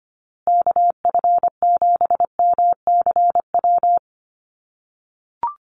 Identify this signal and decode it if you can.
Morse code, heard as a keyed tone, about 25 wpm, 700 Hz. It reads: KF7MCW
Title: USA Call Signs Repeated 2x 25wpm